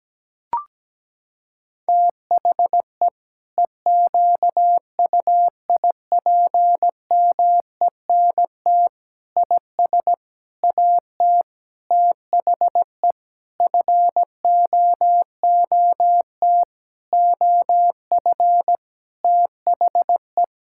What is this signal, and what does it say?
Morse code, heard as a keyed tone, about 17 wpm, 700 Hz. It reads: THE EQUIPMENT IS AT THE FOOT OF THE
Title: Sentences from Top 1000 Words - Review Part 5 - 17wpm